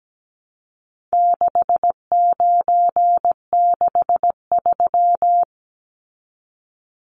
Morse code, heard as a keyed tone, about 17 wpm, 700 Hz. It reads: 6963
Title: Numbers - 4-Digit 17wpm